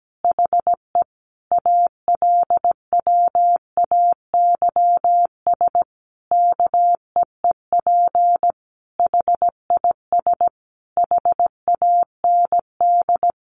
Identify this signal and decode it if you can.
Morse code, heard as a keyed tone, about 17 wpm, 700 Hz. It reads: HE ALWAYS KEEP HIS HAND
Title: Sentences from Top 1200 Words 17wpm